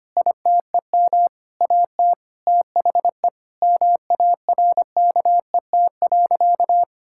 Morse code, heard as a keyed tone, about 25 wpm, 700 Hz. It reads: ITEM AT THE MARKET.
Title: Sentences from Top 1000 Words - Review Part 6 - 25wpm